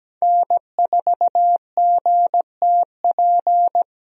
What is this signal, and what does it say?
Morse code, heard as a keyed tone, about 17 wpm, 700 Hz. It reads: N4GTP
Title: USA Call Signs - 5 Characters - Repeated 2x - 17wpm